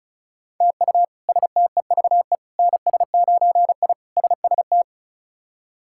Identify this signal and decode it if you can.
Morse code, heard as a keyed tone, about 36 wpm, 700 Hz. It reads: TU STEVE NS9I SST
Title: K1USN SST QSO Practice - 35wpm